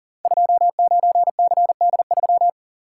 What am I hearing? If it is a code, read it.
Morse code, heard as a keyed tone, about 39 wpm, 700 Hz. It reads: 29CD3
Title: Code Groups - 5 Letter:Numbers - 40wpm